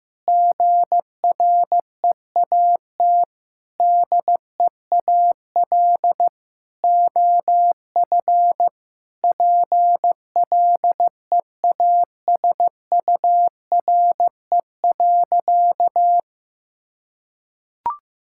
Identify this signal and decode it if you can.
Morse code, heard as a keyed tone, about 15 wpm, 700 Hz. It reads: GREAT DEAL OF PLEASURE.
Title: Sentences from Top 1600 Words 15wpm